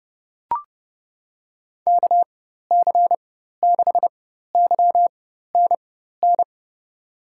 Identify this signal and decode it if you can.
Morse code, heard as a keyed tone, about 31 wpm, spaced as for 14 wpm, 700 Hz. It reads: KC6YNN
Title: USA Call Signs - 6 Characters - 4x Character Spacing - 30wpm